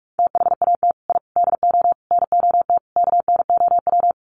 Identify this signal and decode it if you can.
Morse code, heard as a keyed tone, about 44 wpm, 700 Hz. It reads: THAT I DO NOT KNOW
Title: Sets of 5 Words Encore2 45wpm